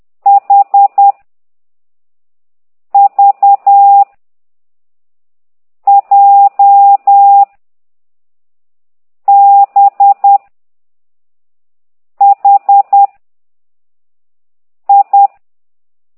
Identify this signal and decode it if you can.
Morse code, heard as a keyed tone, about 10 wpm, 800 Hz. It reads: H V J B H I
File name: ct2idw / exerc15